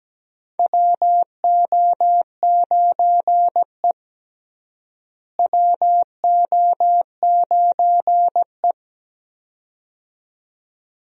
Morse code, heard as a keyed tone, about 17 wpm, 700 Hz. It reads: WO9E WO9E
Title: USA Call Signs - 4 Characters - Repeated 2x - 17wpm